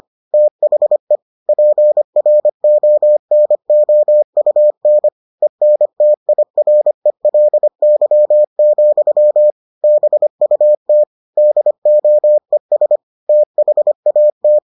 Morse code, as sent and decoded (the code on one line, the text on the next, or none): - .... / .--. .-. --- -. --- ..- -. / . -. - .. .-. .-.. -.-- --..-- / -... ..- - / -.. --- . ... / - .... .- -
THE PRONOUN ENTIRELY, BUT DOES THAT